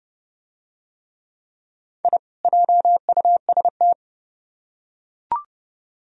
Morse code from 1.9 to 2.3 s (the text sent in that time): I